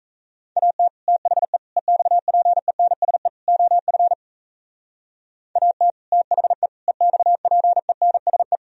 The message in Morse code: AT THE EXPENSE OF AT THE EXPENSE

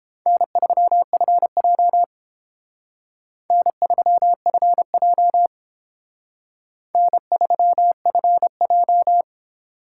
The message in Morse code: N3FJ N3FJ N3FJ

